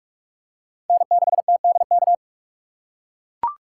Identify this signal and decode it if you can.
Morse code, heard as a keyed tone, about 44 wpm, 700 Hz. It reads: N6TDK
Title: USA Call Signs - 5 Characters - 6x Character Spacing - 45wpm